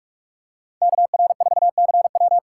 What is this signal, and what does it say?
Morse code, heard as a keyed tone, about 44 wpm, 700 Hz. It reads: KR4CW